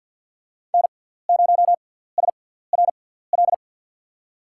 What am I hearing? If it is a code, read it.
Morse code, heard as a keyed tone, about 52 wpm, spaced as for 17 wpm, 700 Hz. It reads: N0SRL
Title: USA Call Signs - 5 Characters - 6x Character Spacing - 50wpm